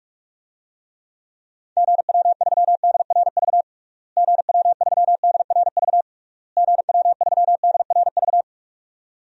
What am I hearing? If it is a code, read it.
Morse code, heard as a keyed tone, about 45 wpm, 700 Hz. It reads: GW3DRV GW3DRV GW3DRV
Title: DXCC-100 Call Signs - 6 Characters - Repeated 3x - 45wpm